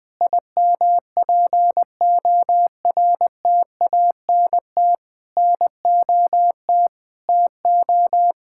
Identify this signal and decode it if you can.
Morse code, heard as a keyed tone, about 19 wpm, 700 Hz. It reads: IMPORTANT NOT TO
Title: Sentences from Top 1000 Words - Review Part 5 - 20wpm